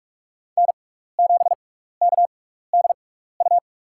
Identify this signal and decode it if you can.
Morse code, heard as a keyed tone, about 41 wpm, spaced as for 15 wpm, 690 Hz. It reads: N7KDU
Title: USA Call Signs - 5 Characters - 6x Character Spacing - 45wpm